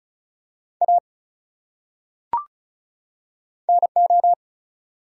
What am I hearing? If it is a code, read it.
Morse code, heard as a keyed tone, about 34 wpm, 700 Hz. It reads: A NO
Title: Top 100 Words 35wpm